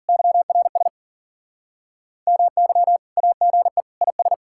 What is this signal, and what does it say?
Morse code, heard as a keyed tone, about 52 wpm, 700 Hz. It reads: YRS MY AGE IS